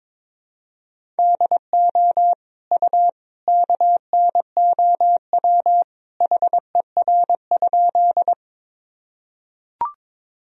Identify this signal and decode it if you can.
Morse code, heard as a keyed tone, about 22 wpm, 700 Hz. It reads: DO U KNOW HER?
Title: Sentences from Top 100 Words and Abbreviations 22wpm